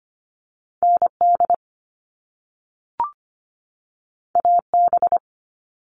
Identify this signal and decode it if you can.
Morse code, heard as a keyed tone, about 25 wpm, 700 Hz. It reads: N D A B
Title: All Cut Numbers - 2-Digit 25wpm